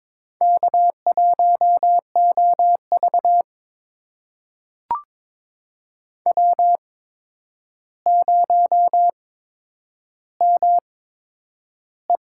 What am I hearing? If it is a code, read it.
Morse code, heard as a keyed tone, about 22 wpm, spaced as for 9 wpm, 700 Hz. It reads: K1OV W0ME
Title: USA Call Signs - 4 Characters - 8x Character Spacing - 22wpm